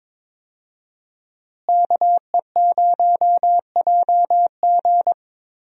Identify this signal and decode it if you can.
Morse code, heard as a keyed tone, about 22 wpm, 700 Hz. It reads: KE0JG
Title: USA Call Signs - 5 Characters - Repeated 3x - 22wpm